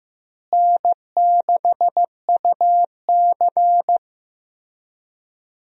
Morse code, heard as a keyed tone, about 16 wpm, 700 Hz. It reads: N6UC
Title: USA Call Signs - 4 Characters - Repeated 3x - 15wpm